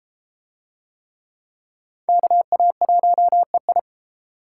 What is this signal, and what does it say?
Morse code, heard as a keyed tone, about 30 wpm, 700 Hz. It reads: KA1EI